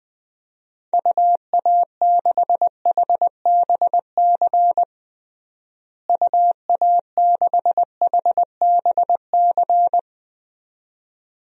Morse code, heard as a keyed tone, about 20 wpm, 700 Hz. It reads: UA6HBC UA6HBC
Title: DXCC-100 Call Signs - 6 Characters - Repeated 2x - 20wpm